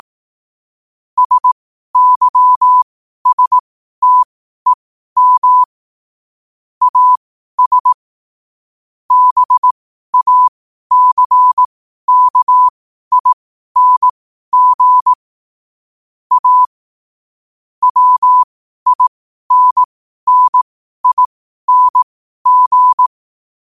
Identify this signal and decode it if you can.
Morse code, heard as a keyed tone, about 18 wpm, spaced as for 12 wpm, 1000 Hz. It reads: SYSTEM AS BACKING A WINNING
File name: qotdfiles / 12wpm